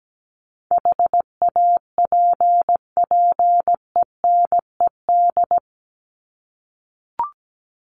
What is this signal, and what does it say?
Morse code, heard as a keyed tone, about 17 wpm, 700 Hz. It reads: HAPPENED
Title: ED Suffix - Top 100 Words with Base Words 17wpm